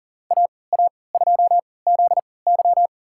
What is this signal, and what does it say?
Morse code, heard as a keyed tone, about 38 wpm, spaced as for 22 wpm, 700 Hz. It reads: AA2ZY
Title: USA Call Signs - 3x Character Spacing - 40wpm